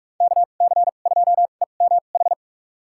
Morse code, heard as a keyed tone, about 46 wpm, spaced as for 33 wpm, 700 Hz. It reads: KC2EMH
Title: USA Call Signs - 6 Characters - 2x Character Spacing - 45wpm